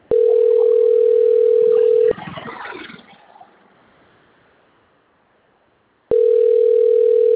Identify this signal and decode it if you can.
Ringback tone (North America), 440 + 480 Hz.